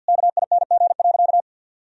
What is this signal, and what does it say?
Morse code, heard as a keyed tone, about 52 wpm, 700 Hz. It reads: KING.
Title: Sentences from Top 1000 Words - Review Part 4 - 50wpm